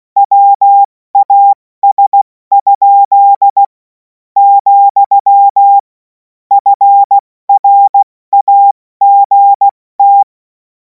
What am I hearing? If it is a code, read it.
Morse code, heard as a keyed tone, about 16 wpm, spaced as for 13 wpm, 800 Hz. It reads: WAS?, FRAGT
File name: Morsekurs / Witze / Witz 07. April 2026 70 F80